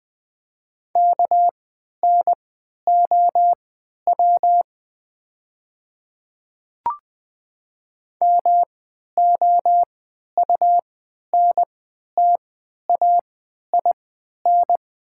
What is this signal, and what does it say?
Morse code, heard as a keyed tone, about 20 wpm, spaced as for 11 wpm, 700 Hz. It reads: KNOW MOUNTAIN
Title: 3x Character Spacing - Top 300 Words - 20wpm